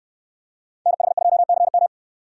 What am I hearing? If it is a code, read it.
Morse code, heard as a keyed tone, about 69 wpm, 700 Hz. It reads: NH2CM